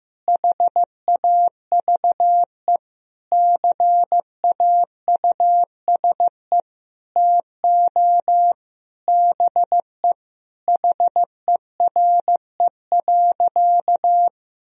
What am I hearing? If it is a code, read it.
Morse code, heard as a keyed tone, about 16 wpm, 700 Hz. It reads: HAVE CAUSE TO BE HERE.
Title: Sentences from Top 1000 Words - Review Part 1 - 15wpm